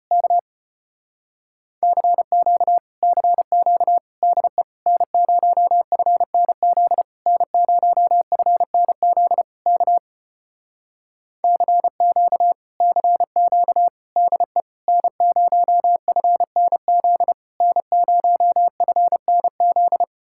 Morse code, heard as a keyed tone, about 33 wpm, 700 Hz. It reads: K CQ CQ DE N0FNZ N0FNZ K CQ CQ DE N0FNZ N0FNZ